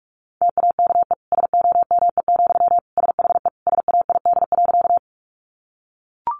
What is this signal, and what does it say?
Morse code, heard as a keyed tone, about 44 wpm, 700 Hz. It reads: TAKE SOME, SHE SAID.
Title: Sentences from Top 200 Words 45wpm